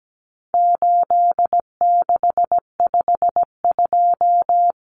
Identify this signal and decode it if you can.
Morse code, heard as a keyed tone, about 17 wpm, 700 Hz. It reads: 8652
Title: Common Cut Numbers - 4-Digit 17wpm